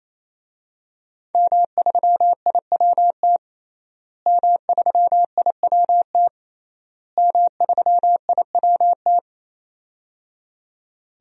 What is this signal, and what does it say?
Morse code, heard as a keyed tone, about 28 wpm, 700 Hz. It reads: M3IWT M3IWT M3IWT
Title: DXCC-100 Call Signs - 5 Characters - Repeated 3x - 28wpm